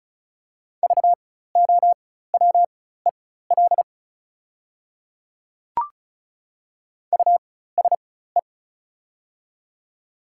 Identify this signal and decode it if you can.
Morse code, heard as a keyed tone, about 36 wpm, spaced as for 16 wpm, 700 Hz. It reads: VOWEL USE
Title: 4x Character Spacing - Top 500 Words - 35wpm